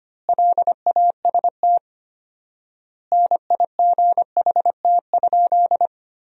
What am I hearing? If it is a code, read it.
Morse code, heard as a keyed tone, about 25 wpm, 700 Hz. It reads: LAST NIGHT?
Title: 4x Word-Spacing - Sentences from Top 300 Words - 25wpm